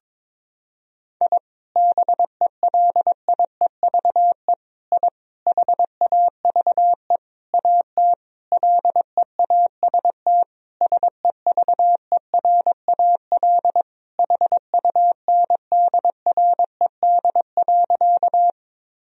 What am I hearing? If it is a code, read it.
Morse code, heard as a keyed tone, about 21 wpm, 700 Hz. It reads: I BELIEVE I HAVE AT LEAST SEVERAL HUNDRED.
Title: Sentences from Top 1000 Words - Review Part 2 - 22wpm